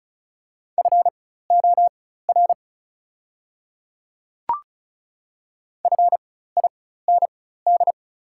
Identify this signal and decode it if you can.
Morse code, heard as a keyed tone, about 34 wpm, spaced as for 16 wpm, 700 Hz. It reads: FOR FIND
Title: 4x Character Spacing - Top 100 Words - 35wpm